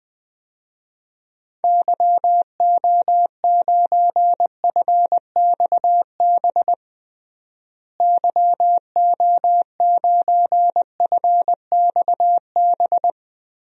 Morse code, heard as a keyed tone, about 20 wpm, 700 Hz. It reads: YO9FXB YO9FXB
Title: DXCC-100 Call Signs - 6 Characters - Repeated 3x - 20wpm